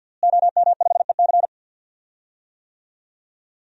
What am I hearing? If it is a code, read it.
Morse code, heard as a keyed tone, about 50 wpm, 700 Hz. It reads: OM5EC
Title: DXCC-100 Call Signs - 5 Characters - Repeated 3x - 50wpm